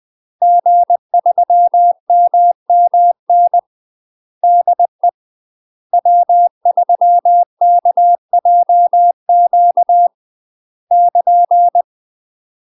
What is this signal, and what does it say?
Morse code, heard as a keyed tone, about 20 wpm, 700 Hz. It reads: G3MMN DE W3KJQ <KN>
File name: UK MASTER QSOs TRAINING / 20WPM / UK MASTER QSOs 0014